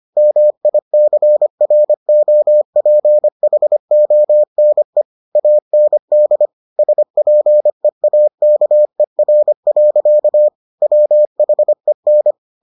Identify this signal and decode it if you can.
Morse code, heard as a keyed tone, about 25 wpm, 600 Hz. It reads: MICROPHONE AND SPEAKER. WHEN